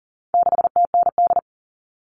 Morse code, heard as a keyed tone, about 40 wpm, 700 Hz. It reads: 6TND
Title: All Cut Numbers - 4-Digit 40wpm